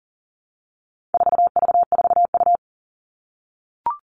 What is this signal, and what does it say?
Morse code, heard as a keyed tone, about 37 wpm, 700 Hz. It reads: 4V4U
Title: All Cut Numbers - 4-Digit 40wpm